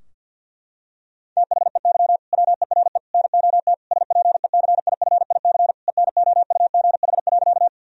Morse code, heard as a keyed tone, about 50 wpm, 700 Hz. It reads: THEY WERE NOT SPECIFIC ENOUGH.
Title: Sentences from Top 1000 Words - Review Part 3 - 50wpm